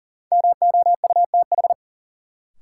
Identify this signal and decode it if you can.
Morse code, heard as a keyed tone, about 40 wpm, 700 Hz. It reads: MOUTH
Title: Sentences from Top 1500 Words - New Word Drill Alt - 40wpm